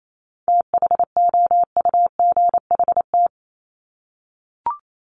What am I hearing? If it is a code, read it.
Morse code, heard as a keyed tone, about 28 wpm, 700 Hz. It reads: THOUGHT
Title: Top 500 Words 28wpm